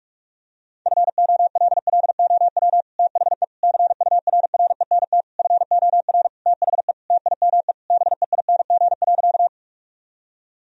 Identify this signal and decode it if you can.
Morse code, heard as a keyed tone, about 41 wpm, 700 Hz. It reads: FOLLOW THE CURRENT FOR THE TIME BEING.
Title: Sentences from Top 1000 Words - Review Part 4 - 45wpm